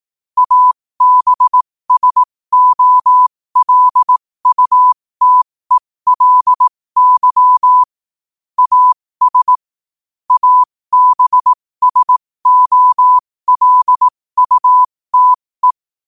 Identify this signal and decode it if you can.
Morse code, heard as a keyed tone, about 18 wpm, spaced as for 15 wpm, 1000 Hz. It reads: ABSOLUTELY AS ABSOLUTE
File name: qotdfiles / 15wpm